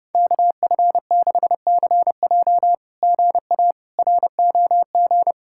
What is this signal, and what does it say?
Morse code, heard as a keyed tone, about 30 wpm, 700 Hz. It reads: KF6CJ GA ROG
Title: SOTA QSO Practice 30wpm